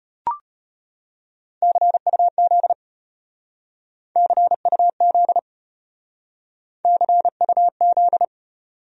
Morse code, heard as a keyed tone, about 33 wpm, 700 Hz. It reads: CUZ CUZ CUZ